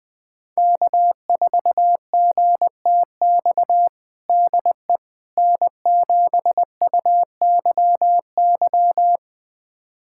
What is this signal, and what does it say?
Morse code, heard as a keyed tone, about 20 wpm, 700 Hz. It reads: K4GTX DE N7UYY
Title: USA Callsign DE Callsign 20wpm